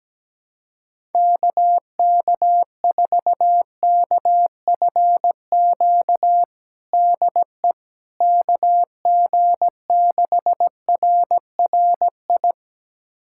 Morse code, heard as a keyed tone, about 17 wpm, 700 Hz. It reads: KK4KFQ DE KG6RRI
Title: USA Callsign DE Callsign - 6 Characters - 17wpm